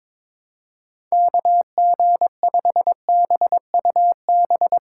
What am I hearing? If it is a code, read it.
Morse code, heard as a keyed tone, about 24 wpm, 700 Hz. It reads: KG5BUB
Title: Speed Racing - USA Call Signs 22wpm